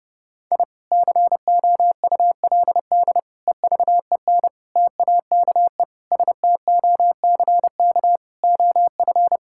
Morse code, heard as a keyed tone, about 32 wpm, 700 Hz. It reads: I COULD EVEN TAKE STOCK OF
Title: Sentences from Top 1000 Words - Review Part 5 - 30wpm